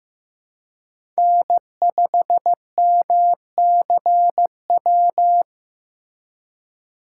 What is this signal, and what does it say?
Morse code, heard as a keyed tone, about 18 wpm, 700 Hz. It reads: N5MCW
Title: Speed Racing - USA Call Signs 15wpm